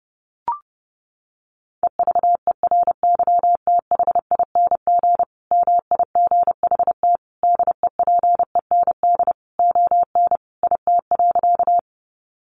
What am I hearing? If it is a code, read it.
Morse code, heard as a keyed tone, about 30 wpm, 700 Hz. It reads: EVERYTHING MIGHT DEPEND ON IT.
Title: Sentences from Top 1300 Words 30wpm